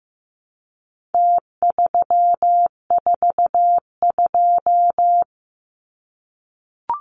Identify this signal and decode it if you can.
Morse code, heard as a keyed tone, about 15 wpm, 700 Hz. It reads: T342
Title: Numbers - Contest Style - 4-Digit 15wpm